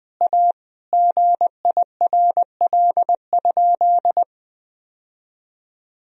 Morse code, heard as a keyed tone, about 19 wpm, 700 Hz. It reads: A GIRL?
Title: Sentences from Top 1000 Words - Review Part 1 - 20wpm